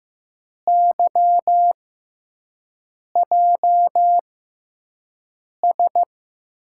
Morse code, heard as a keyed tone, about 15 wpm, spaced as for 5 wpm, 700 Hz. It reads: YJS
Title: USA Call Signs - 6 Characters - 6x Character Spacing - 15wpm